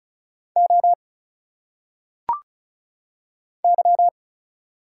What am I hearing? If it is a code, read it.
Morse code, heard as a keyed tone, about 36 wpm, 700 Hz. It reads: O Y
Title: Single Letters - Phonetic Pronunciation 35wpm